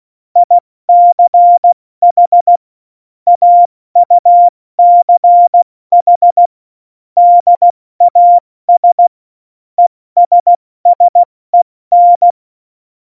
Morse code, heard as a keyed tone, about 16 wpm, 700 Hz. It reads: ICH AUCH DAS ESSEN
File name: Morsekurs / Witze / Witz 12. Februar 2019, 70 F80